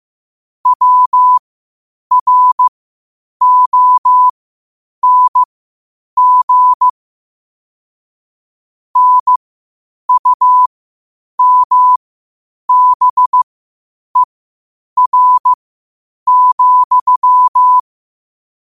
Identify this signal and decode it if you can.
Morse code, heard as a keyed tone, about 15 wpm, spaced as for 8 wpm, 1000 Hz. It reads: WRONG NUMBER,
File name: 8wpm